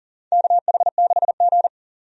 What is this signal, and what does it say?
Morse code, heard as a keyed tone, about 40 wpm, 700 Hz. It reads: KH6G
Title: DXCC-100 Call Signs - 4 Characters - 40wpm